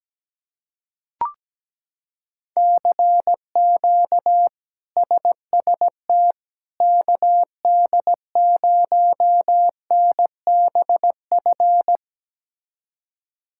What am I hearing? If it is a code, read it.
Morse code, heard as a keyed tone, about 17 wpm, 700 Hz. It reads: CQ SST KD0NBF
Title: K1USN SST QSO Practice - 17wpm